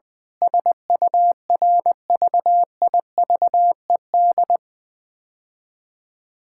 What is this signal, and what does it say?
Morse code, heard as a keyed tone, about 20 wpm, 700 Hz. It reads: SURVIVED
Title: Sentences with 7-Word Phrases - 20wpm